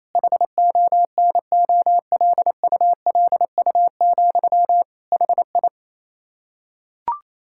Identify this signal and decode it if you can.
Morse code, heard as a keyed tone, about 28 wpm, 700 Hz. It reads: HONOLULU, HI
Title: Practice QSO - Exchange 1 - Beginning - 28wpm